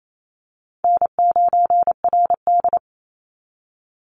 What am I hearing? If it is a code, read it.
Morse code, heard as a keyed tone, about 28 wpm, 700 Hz. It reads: N9RD
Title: Easy Call Signs - 28wpm